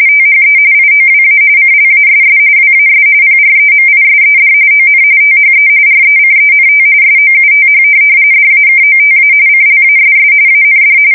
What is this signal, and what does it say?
RTTY, about 45.45 baud. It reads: THE IOWA CLASS WAS PREMISED ON THE RECENT SOUTH DAKOTA CLASS AND
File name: fsk170 / USS-New-Jersey